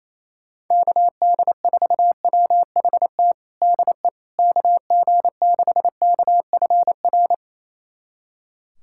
Morse code, heard as a keyed tone, about 28 wpm, 700 Hz. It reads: KD4WHT DE KG6KFR
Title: USA Callsign DE Callsign - 6 Characters - 28wpm